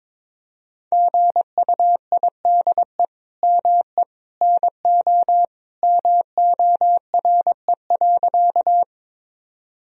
Morse code, heard as a keyed tone, about 22 wpm, 700 Hz. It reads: GUIDE ME NO MORE.
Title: Sentences from Top 1000 Words - Review Part 3 - 22wpm